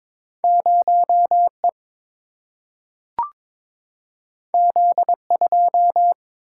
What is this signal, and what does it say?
Morse code, heard as a keyed tone, about 22 wpm, 700 Hz. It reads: 0E Z2
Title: Single-Letter Single-Number 22wpm